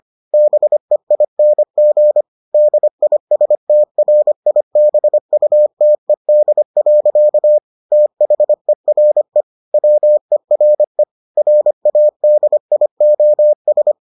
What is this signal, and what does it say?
Morse code, heard as a keyed tone, about 25 wpm, 600 Hz. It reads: BEING DISTRIBUTED. THERE WERE RADIOS